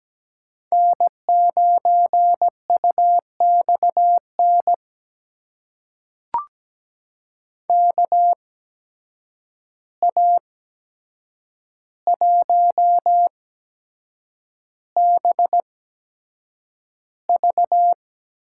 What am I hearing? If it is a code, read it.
Morse code, heard as a keyed tone, about 17 wpm, spaced as for 7 wpm, 700 Hz. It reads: N9UXN KA1BV
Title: USA Call Signs - 8x Character Spacing - 17wpm